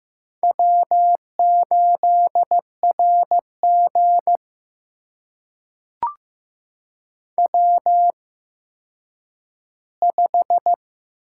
Morse code, heard as a keyed tone, about 15 wpm, spaced as for 9 wpm, 700 Hz. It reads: W8RG W5
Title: USA Call Signs - 4 Characters - 8x Character Spacing - 15wpm